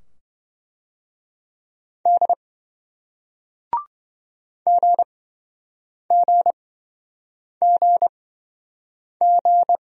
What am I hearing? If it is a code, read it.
Morse code, heard as a keyed tone, about 26 wpm, spaced as for 7 wpm, 700 Hz. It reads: D GGGG